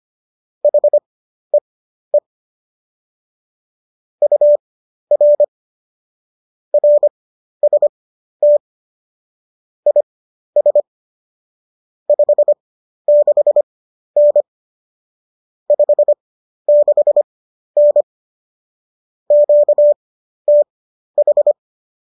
Morse code, heard as a keyed tone, about 25 wpm, spaced as for 12 wpm, 590 Hz. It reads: HEE UR RST IS 56N 56N QTH